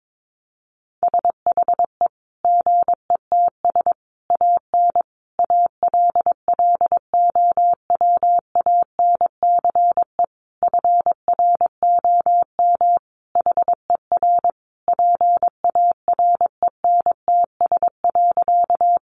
Morse code, heard as a keyed tone, about 22 wpm, 700 Hz. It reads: SHE GETS AN ALLOWANCE FROM HER PARENTS.
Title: Easy Phrases 22wpm